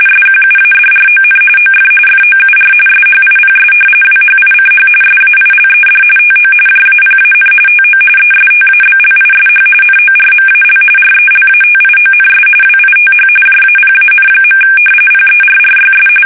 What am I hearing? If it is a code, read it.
RTTY, about 45.45 baud. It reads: SQUADRON OF FOUR SHIPS WHICH SAILED TO JAPAN TO ESTABLISH NORMAL RELATIONS WITH JAPAN. PERRYS
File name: fsk850 / usnavy2